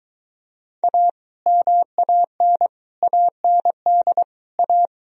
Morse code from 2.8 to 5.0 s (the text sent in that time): AND A